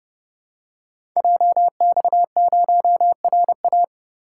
Morse code, heard as a keyed tone, about 30 wpm, 700 Hz. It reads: JX0RA
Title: Code Groups - 5 Letter:Numbers - 30wpm